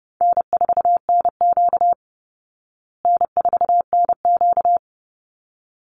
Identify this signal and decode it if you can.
Morse code, heard as a keyed tone, about 28 wpm, 700 Hz. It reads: N4NQ N4NQ
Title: Easy Call Signs Repeated 2x - 30wpm